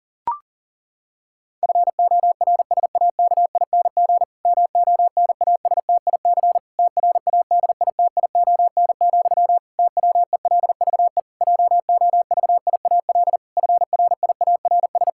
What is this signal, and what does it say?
Morse code, heard as a keyed tone, about 37 wpm, 700 Hz. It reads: FORSAKING MONASTIC TRADITION, TWELVE JOVIAL FRIARS
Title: Pangram Challenge v2 - 40wpm